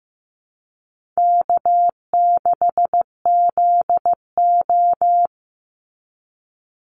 Morse code, heard as a keyed tone, about 15 wpm, 700 Hz. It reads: K6ZO
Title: Easy Call Signs Repeated 3x - 15wpm